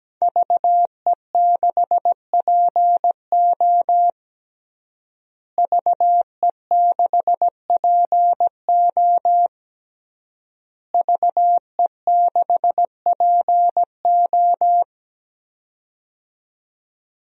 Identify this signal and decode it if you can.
Morse code, heard as a keyed tone, about 17 wpm, 700 Hz. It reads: VE6PO VE6PO VE6PO
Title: DXCC-100 Call Signs - 5 Characters - Repeated 3x - 17wpm